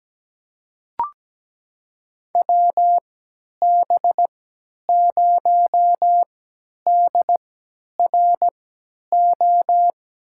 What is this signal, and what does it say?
Morse code, heard as a keyed tone, about 17 wpm, spaced as for 10 wpm, 700 Hz. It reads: WB0DRO
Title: USA Call Signs - 6 Characters - 3x Character Spacing - 17wpm